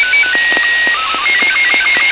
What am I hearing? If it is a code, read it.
SSTV (slow-scan television), Robot 36.
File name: co8tw / Arissat1 28oct11 1727